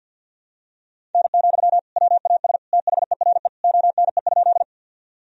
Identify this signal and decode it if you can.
Morse code, heard as a keyed tone, about 51 wpm, 700 Hz. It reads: N, WAS THERE ONE?
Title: Sentences from Top 100 Words and Abbreviations 50wpm